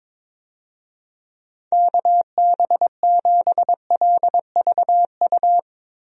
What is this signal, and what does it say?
Morse code, heard as a keyed tone, about 22 wpm, 700 Hz. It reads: KB7LVU
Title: USA Call Signs - 6 Characters - Repeated 3x - 22wpm